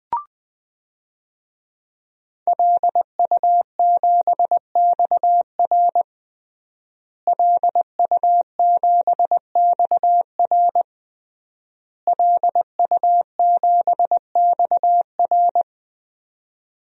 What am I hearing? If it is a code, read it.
Morse code, heard as a keyed tone, about 20 wpm, 700 Hz. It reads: LU7XR LU7XR LU7XR
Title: DXCC-100 Call Signs - 5 Characters - Repeated 3x - 20wpm